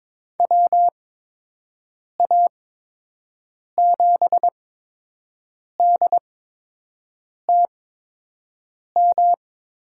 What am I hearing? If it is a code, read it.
Morse code, heard as a keyed tone, about 22 wpm, spaced as for 6 wpm, 700 Hz. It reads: WA7DTM
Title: USA Call Signs - 6 Characters - 8x Character Spacing - 22wpm